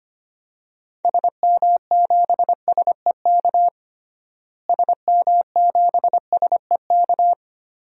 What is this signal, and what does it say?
Morse code, heard as a keyed tone, about 25 wpm, 700 Hz. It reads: SM7SEK SM7SEK
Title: DXCC-100 Call Signs - 6 Characters - Repeated 3x - 25wpm